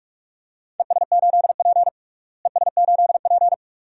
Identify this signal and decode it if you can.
Morse code, heard as a keyed tone, about 46 wpm, 700 Hz. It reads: ES8P ES8P
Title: DXCC-100 Call Signs - 4 Characters - Repeated 2x - 45wpm